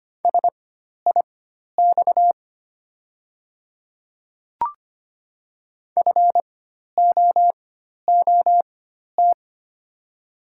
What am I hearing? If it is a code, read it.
Morse code, heard as a keyed tone, about 25 wpm, spaced as for 12 wpm, 700 Hz. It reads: SIX FOOT
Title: 4x Character Spacing - Top 500 Words - 25wpm